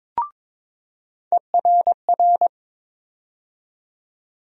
Morse code, heard as a keyed tone, about 21 wpm, 700 Hz. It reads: ERR
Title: USA Call Sign Postfixes - 3 Letters 22wpm